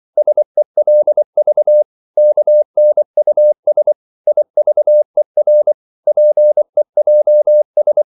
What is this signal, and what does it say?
Morse code, heard as a keyed tone, about 24 wpm, 600 Hz. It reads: SELV KNUS IVER PEJS
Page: Korte danske ord 24wpm